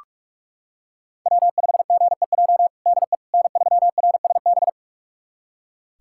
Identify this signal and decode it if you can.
Morse code, heard as a keyed tone, about 46 wpm, 700 Hz. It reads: W5GEJ DE N3RSB
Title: USA Callsign DE Callsign - 5 Characters - 45wpm